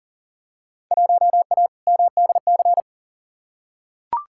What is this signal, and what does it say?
Morse code, heard as a keyed tone, about 40 wpm, 700 Hz. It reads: 1A MDC
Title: Field Day Exchanges - 40wpm